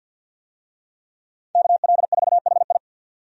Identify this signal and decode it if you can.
Morse code, heard as a keyed tone, about 50 wpm, 700 Hz. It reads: KL4HI
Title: USA Call Signs - 5 Characters - 50wpm